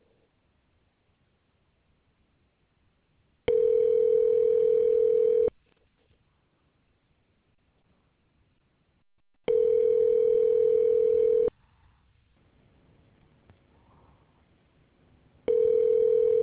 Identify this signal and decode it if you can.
Ringback tone (North America), 440 + 480 Hz.